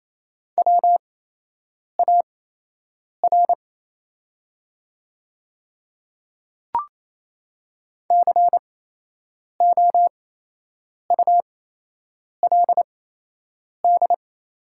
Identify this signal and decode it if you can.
Morse code, heard as a keyed tone, about 27 wpm, spaced as for 8 wpm, 690 Hz. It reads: WAR COULD
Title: 8x Character Spacing - Top 500 Words - 28wpm